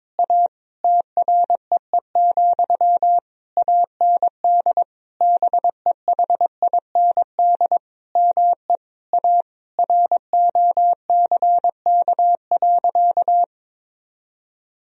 Morse code, heard as a keyed tone, about 22 wpm, 700 Hz. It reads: A TREE, AND BEHIND ME A ROCK.
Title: Sentences from Top 1000 Words - Review Part 2 - 22wpm